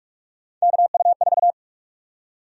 Morse code, heard as a keyed tone, about 46 wpm, 700 Hz. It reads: KU4
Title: USA Call Sign Prefixes - 45wpm